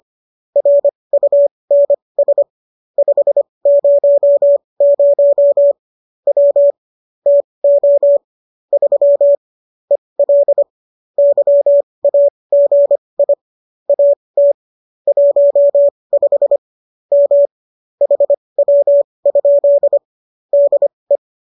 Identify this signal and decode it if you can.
Morse code, heard as a keyed tone, about 25 wpm, spaced as for 20 wpm, 570 Hz. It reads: RUNS 500 W TO 3 EL YAGI AT 15 M HW? DE